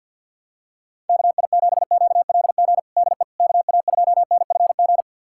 Morse code, heard as a keyed tone, about 53 wpm, 700 Hz. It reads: KI7QLG DE KA2NFG
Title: USA Callsign DE Callsign - 6 Characters - 50wpm